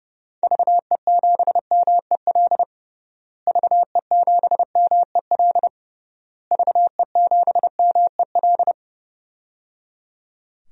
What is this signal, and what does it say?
Morse code, heard as a keyed tone, about 30 wpm, 700 Hz. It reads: VE7MEL VE7MEL VE7MEL
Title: DXCC-100 Call Signs - 6 Characters - Repeated 3x - 30wpm